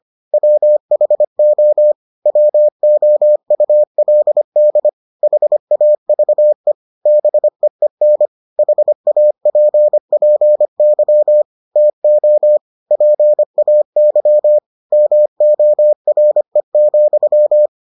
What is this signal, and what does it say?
Morse code, heard as a keyed tone, about 25 wpm, 600 Hz. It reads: WHO WOULD HAVE BEEN HAPPY TO PAY MORE,